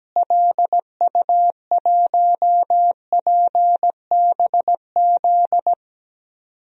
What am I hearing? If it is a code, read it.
Morse code, heard as a keyed tone, about 17 wpm, 700 Hz. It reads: LU1PBZ
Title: DXCC-100 Call Signs - 6 Characters - Repeated 2x - 17wpm